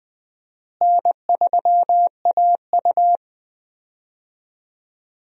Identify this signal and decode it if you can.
Morse code, heard as a keyed tone, about 20 wpm, 700 Hz. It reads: N3AU
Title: USA Call Signs - 4 Characters - Repeated 3x - 20wpm